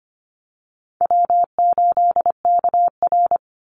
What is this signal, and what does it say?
Morse code, heard as a keyed tone, about 25 wpm, 700 Hz. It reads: W8KR
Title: Easy Call Signs - 25wpm